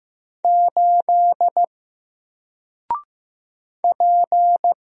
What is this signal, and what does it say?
Morse code, heard as a keyed tone, about 15 wpm, 700 Hz. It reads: 8 P
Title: Single Character (Letter:Number:Punctuation:Prosign) Alt 15wpm